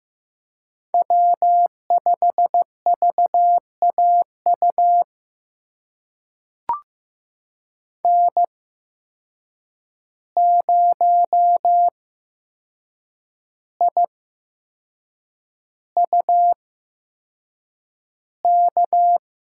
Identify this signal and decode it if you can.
Morse code, heard as a keyed tone, about 15 wpm, spaced as for 6 wpm, 700 Hz. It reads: W5VAU N0IUK
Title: USA Call Signs - 5 Characters - 8x Character Spacing - 15wpm